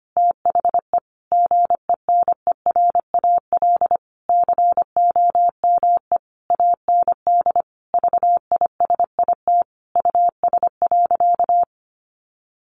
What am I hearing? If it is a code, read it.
Morse code, heard as a keyed tone, about 25 wpm, 700 Hz. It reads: THE GENERAL COME AND VISIT US.
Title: Sentences from Top 1400 Words 25wpm